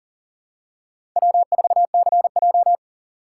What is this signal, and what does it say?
Morse code, heard as a keyed tone, about 37 wpm, 700 Hz. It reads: W4CJ
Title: USA Call Signs - 3x Character Spacing - 40wpm